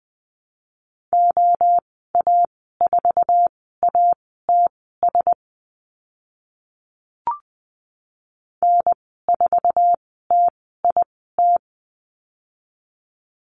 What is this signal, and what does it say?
Morse code, heard as a keyed tone, about 20 wpm, spaced as for 14 wpm, 700 Hz. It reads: OA4ATS N4TIT
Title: 069 Lesson 12g - Letter d - Call Signs 20wpm